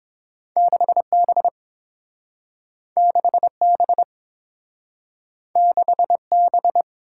Morse code, heard as a keyed tone, about 30 wpm, 700 Hz. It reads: 6B 6B 6B